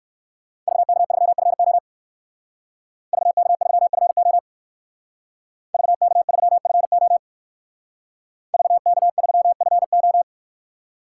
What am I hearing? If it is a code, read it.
Morse code, heard as a keyed tone, about 59 wpm, 700 Hz. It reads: VK3FO VK3FO VK3FO VK3FO